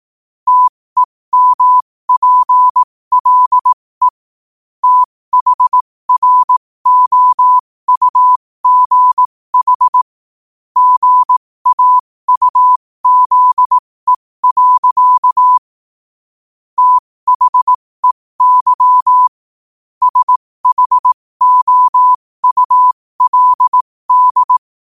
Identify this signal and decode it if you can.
Morse code, heard as a keyed tone, about 18 wpm, spaced as for 15 wpm, 1000 Hz. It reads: TEMPLE THROUGH GAUZE. THEY SHOULD
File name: qotdfiles / 15wpm